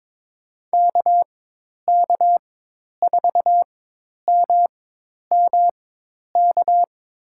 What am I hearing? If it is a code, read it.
Morse code, heard as a keyed tone, about 21 wpm, spaced as for 10 wpm, 700 Hz. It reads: KK4MMK
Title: USA Call Signs - 6 Characters - 4x Character Spacing - 22wpm